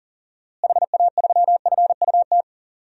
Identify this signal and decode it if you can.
Morse code, heard as a keyed tone, about 40 wpm, 700 Hz. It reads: HA3FUT
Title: DXCC-100 Call Signs - 6 Characters - Repeated 2x - 40wpm